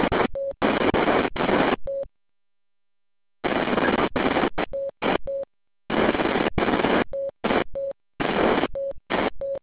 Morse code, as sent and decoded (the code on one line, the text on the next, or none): none